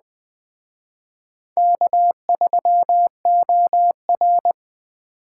Morse code, heard as a keyed tone, about 20 wpm, 700 Hz. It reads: K3OR
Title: Speed Racing - USA 1x2 Call Signs 17wpm